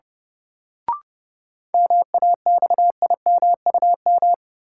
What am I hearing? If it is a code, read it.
Morse code, heard as a keyed tone, about 31 wpm, 700 Hz. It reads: MAXIMUM
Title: Sentences from Top 1300 Words - New Word Drill Alt - 30wpm